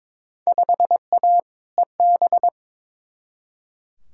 Morse code, heard as a keyed tone, about 22 wpm, 700 Hz. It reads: 5A EB
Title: Field Day Exchanges Encore - 22wpm